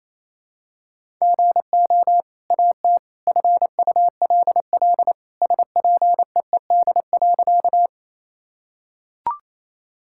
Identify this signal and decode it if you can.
Morse code, heard as a keyed tone, about 28 wpm, 700 Hz. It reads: GO AT FULL SPEED.
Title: Sentences from Top 1000 Words - Review Part 5 - 28wpm